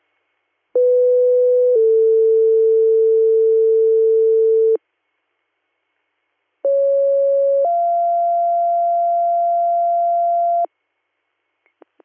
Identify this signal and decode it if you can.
Two-tone page, heard as a one-second tone, then a three-second tone, about 500 then 450 Hz, about 550 then 690 Hz. It is heard from 0.8 to 10.7 s.